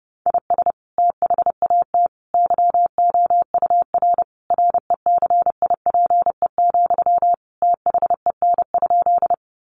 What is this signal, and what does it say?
Morse code, heard as a keyed tone, about 30 wpm, 700 Hz. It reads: IS THAT YOUR RECIPE, THEN?
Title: Sentences from Top 1200 Words 30wpm